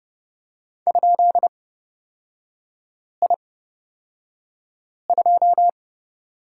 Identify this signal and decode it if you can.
Morse code, heard as a keyed tone, about 29 wpm, spaced as for 5 wpm, 690 Hz. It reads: ?I2
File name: course / 30:10wpm / 236 Lesson 38e - Letter q - Three Characters 30wpm